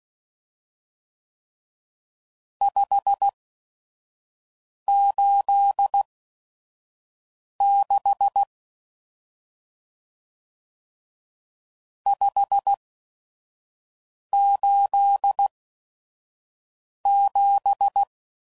Morse code, heard as a keyed tone, about 16 wpm, spaced as for 5 wpm, 800 Hz. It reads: 586 587
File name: CW Code Groups @5WPM / RST's / Various RST No's